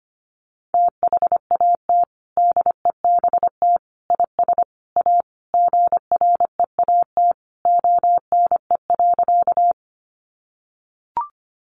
Morse code, heard as a keyed tone, about 25 wpm, 700 Hz. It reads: THAT DEBT IS A GREAT ONE.
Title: Sentences from Top 1300 Words 25wpm